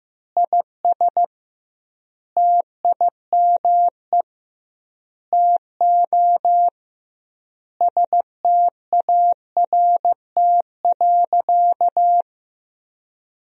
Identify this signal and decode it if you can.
Morse code, heard as a keyed tone, about 15 wpm, 700 Hz. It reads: IS TIME TO START.
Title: 2x Word-Spacing - Sentences from Top 300 Words - 15wpm